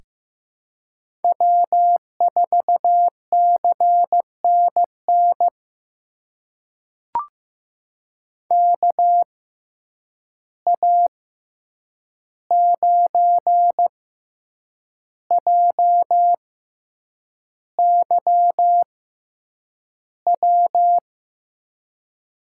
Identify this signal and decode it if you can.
Morse code, heard as a keyed tone, about 15 wpm, spaced as for 7 wpm, 700 Hz. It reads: W4CNN KA9JYW
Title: USA Call Signs - 6x Character Spacing - 15wpm